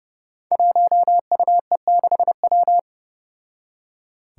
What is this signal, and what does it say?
Morse code, heard as a keyed tone, about 32 wpm, 700 Hz. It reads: 1UE6W
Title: Five Challenging Letter-Number Combinations 30wpm